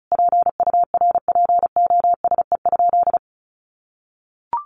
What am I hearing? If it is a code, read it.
Morse code, heard as a keyed tone, about 34 wpm, 700 Hz. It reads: PURPOSE?
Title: Sentences from Top 1500 Words 35wpm